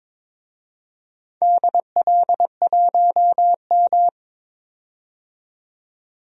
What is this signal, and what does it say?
Morse code, heard as a keyed tone, about 22 wpm, 700 Hz. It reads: DL1M
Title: DXCC-100 Call Signs - 4 Characters - 22wpm